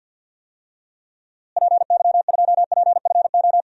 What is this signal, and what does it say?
Morse code, heard as a keyed tone, about 52 wpm, 700 Hz. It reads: PY2PFO
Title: DXCC-100 Call Signs - 6 Characters - Repeated 2x - 50wpm